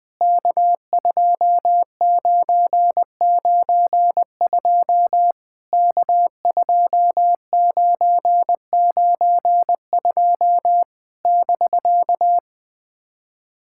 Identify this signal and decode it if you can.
Morse code, heard as a keyed tone, about 20 wpm, 700 Hz. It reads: K2992 K2992 <BK>
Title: POTA QSO Practice - 20wpm